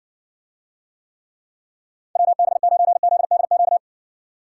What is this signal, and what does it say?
Morse code, heard as a keyed tone, about 60 wpm, 700 Hz. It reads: WB9ZDQ